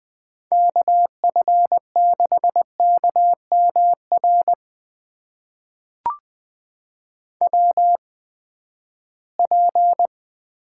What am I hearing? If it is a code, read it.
Morse code, heard as a keyed tone, about 20 wpm, spaced as for 14 wpm, 700 Hz. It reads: KF6KMR WP
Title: USA Call Signs - 6 Characters - 8x Character Spacing - 20wpm